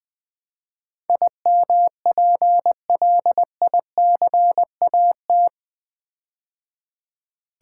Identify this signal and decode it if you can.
Morse code, heard as a keyed tone, about 20 wpm, 700 Hz. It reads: IMPLICAT
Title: Build A Word - 4 Letters At A Time - 11-Letter Words - 20wpm